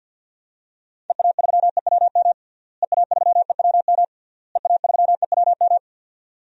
Morse code, heard as a keyed tone, about 47 wpm, 700 Hz. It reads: EA3EWM EA3EWM EA3EWM
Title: DXCC-100 Call Signs - 6 Characters - Repeated 3x - 50wpm